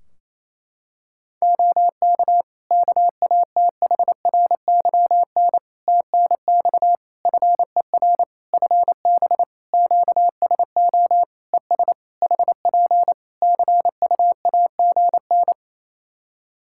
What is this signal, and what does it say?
Morse code, heard as a keyed tone, about 28 wpm, 700 Hz. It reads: OK KATHRYN TNX FER FB QSO ES HP CUAGN
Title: K1USN SST QSO Practice 28wpm